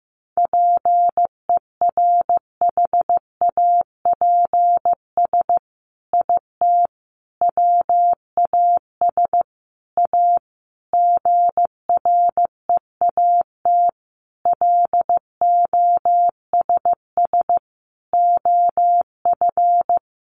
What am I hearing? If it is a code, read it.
Morse code, heard as a keyed tone, about 15 wpm, 700 Hz. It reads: PERHAPS IT WAS A GREAT LOSS OF
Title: Sentences from Top 1400 Words 15wpm